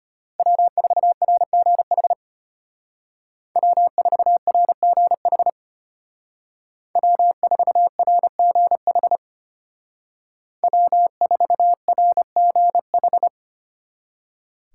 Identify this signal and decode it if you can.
Morse code, heard as a keyed tone, about 31 wpm, 700 Hz. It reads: W4RGH W4RGH W4RGH W4RGH